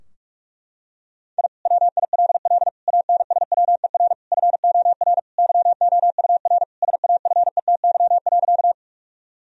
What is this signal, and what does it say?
Morse code, heard as a keyed tone, about 46 wpm, 700 Hz. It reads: I WILL ANSWER FOR YOUR SAFETY.
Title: Sentences from Top 1000 Words - Review Part 4 - 45wpm